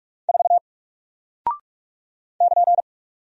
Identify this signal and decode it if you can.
Morse code, heard as a keyed tone, about 43 wpm, 700 Hz. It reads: 4 <KN>
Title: Single Character (Letter:Number:Punctuation:Prosign) Alt 45wpm